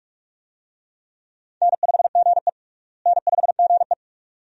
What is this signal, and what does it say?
Morse code, heard as a keyed tone, about 42 wpm, 700 Hz. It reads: N5GE N5GE
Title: USA Call Signs - 4 Characters - Repeated 2x - 45wpm